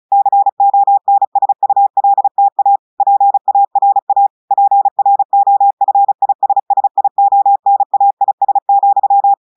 Morse code, heard as a keyed tone, about 36 wpm, 800 Hz. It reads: CONSULTA PARA PROFISSIONAIS,
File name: audios / portugais / portugais texte en clair 1 vitesse 35 mots minutes